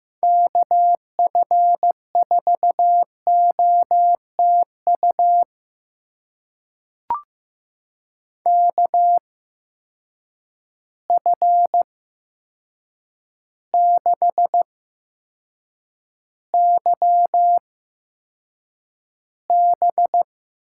Morse code, heard as a keyed tone, about 15 wpm, spaced as for 7 wpm, 700 Hz. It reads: KF4OTU KF6YB
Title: USA Call Signs - 6 Characters - 8x Character Spacing - 15wpm